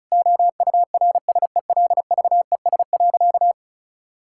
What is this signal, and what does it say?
Morse code, heard as a keyed tone, about 34 wpm, 700 Hz. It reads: OURSELVES.
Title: Sentences from Top 1600 Words 35wpm